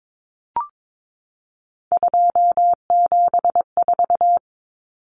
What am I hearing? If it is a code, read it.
Morse code, heard as a keyed tone, about 22 wpm, 700 Hz. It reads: 274
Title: Numbers - 3-Digit 22wpm